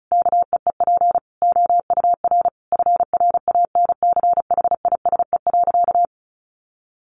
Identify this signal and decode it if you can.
Morse code, heard as a keyed tone, about 34 wpm, 700 Hz. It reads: KEEP OUR FRANCHISE.
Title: A Sound Of Thunder 35wpm